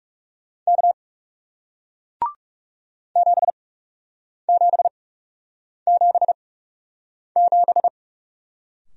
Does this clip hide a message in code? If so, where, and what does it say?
Yes: Morse code from 0.7 to 7.9 s, heard as a keyed tone, about 40 wpm, spaced as for 8 wpm, 700 Hz. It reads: K 7777